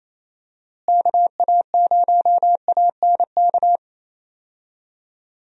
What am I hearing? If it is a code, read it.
Morse code, heard as a keyed tone, about 27 wpm, 700 Hz. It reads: KA0ANK
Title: USA Call Signs - 6 Characters - Repeated 3x - 28wpm